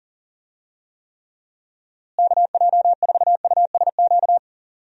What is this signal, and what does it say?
Morse code, heard as a keyed tone, about 40 wpm, 700 Hz. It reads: KJ4USQ
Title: USA Call Signs - 6 Characters - Repeated 2x - 40wpm